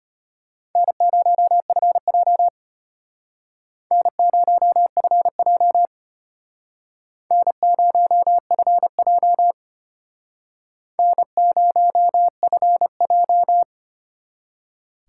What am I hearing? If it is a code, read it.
Morse code, heard as a keyed tone, about 33 wpm, 700 Hz. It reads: N0FJ N0FJ N0FJ N0FJ